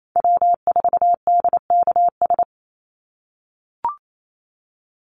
Morse code, heard as a keyed tone, about 28 wpm, 700 Hz. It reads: W4DKS
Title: Call Signs - 28wpm